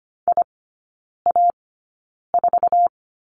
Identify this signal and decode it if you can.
Morse code, heard as a keyed tone, about 25 wpm, spaced as for 9 wpm, 690 Hz. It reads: IA4
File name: Lesson 15e - Number 5 - Three Characters 25wpm